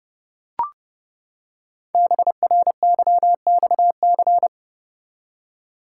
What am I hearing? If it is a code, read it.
Morse code, heard as a keyed tone, about 30 wpm, 700 Hz. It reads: BRYXC
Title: Code Groups - 5 Letters - 30wpm